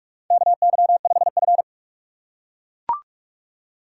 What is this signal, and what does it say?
Morse code, heard as a keyed tone, about 44 wpm, 700 Hz. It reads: KY5F
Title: USA Call Signs - 4 Characters - 4x Character Spacing - 45wpm